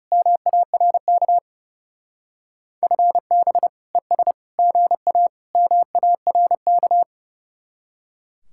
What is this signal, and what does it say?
Morse code, heard as a keyed tone, about 39 wpm, 700 Hz. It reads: MARK FB ES GA MARK